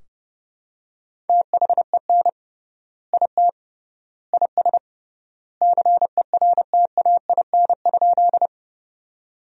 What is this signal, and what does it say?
Morse code, heard as a keyed tone, about 32 wpm, 700 Hz. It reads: THEN IT IS CERTAIN?
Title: 3x Word-Spacing - Sentences from Top 500 Words - 30wpm